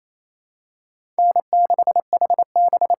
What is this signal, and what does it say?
Morse code, heard as a keyed tone, about 28 wpm, 700 Hz. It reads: N6HB
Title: USA Call Signs - 5 Characters - 6x Character Spacing - 28wpm